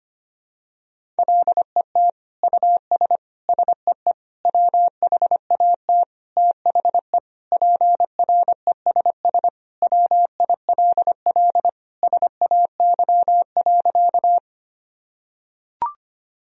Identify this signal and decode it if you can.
Morse code, heard as a keyed tone, about 25 wpm, 700 Hz. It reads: LET US SEE WHAT THE PRESS WILL SAY.
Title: Sentences from Top 1000 Words - Review Part 1 - 25wpm